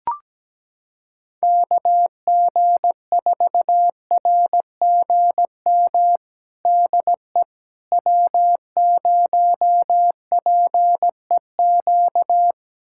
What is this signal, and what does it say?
Morse code, heard as a keyed tone, about 17 wpm, 700 Hz. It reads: KG4RGM DE W0PEQ
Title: USA Callsign DE Callsign 17wpm